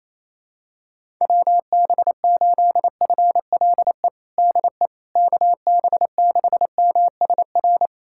Morse code, heard as a keyed tone, about 28 wpm, 700 Hz. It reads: WB8FLE DE KB6MSR
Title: USA Callsign DE Callsign - 6 Characters - 28wpm